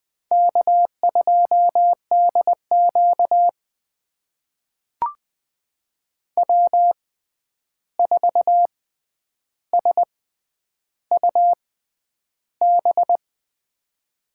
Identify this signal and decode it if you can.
Morse code, heard as a keyed tone, about 21 wpm, spaced as for 10 wpm, 700 Hz. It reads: K2DQ W4SUB
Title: USA Call Signs - 6x Character Spacing - 20wpm